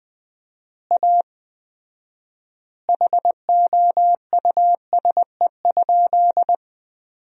Morse code, heard as a keyed tone, about 20 wpm, 700 Hz. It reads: A HOUSE?
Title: 4x Word-Spacing - Sentences from Top 200 Words - 20wpm